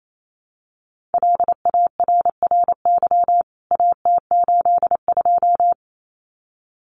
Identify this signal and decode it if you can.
Morse code, heard as a keyed tone, about 28 wpm, 700 Hz. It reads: LARRY AT82
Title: CWT Exchange 28wpm